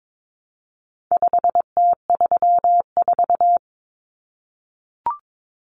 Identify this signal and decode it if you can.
Morse code, heard as a keyed tone, about 22 wpm, 700 Hz. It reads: 5T34
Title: Common Cut Numbers - 4-Digit 22wpm